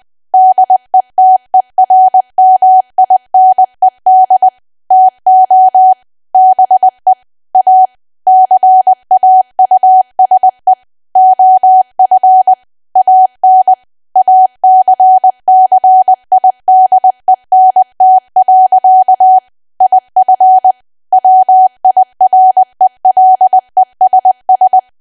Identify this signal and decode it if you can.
Morse code, heard as a keyed tone, about 20 wpm, 750 Hz. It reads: DETERMINED TO BE A CAUSE OF AN ACCIDENT. IF WIRELESS
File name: cwa / W1AW-Level3 / 20WPM